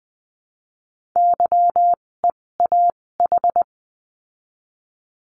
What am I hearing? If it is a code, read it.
Morse code, heard as a keyed tone, about 19 wpm, spaced as for 16 wpm, 700 Hz. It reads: YEAH
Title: 256 Lesson 42a - Words 12-20wpm